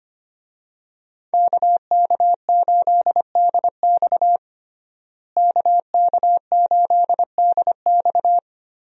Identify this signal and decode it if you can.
Morse code, heard as a keyed tone, about 25 wpm, 700 Hz. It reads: KK8DX KK8DX
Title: USA Call Signs - 5 Characters - Repeated 2x - 25wpm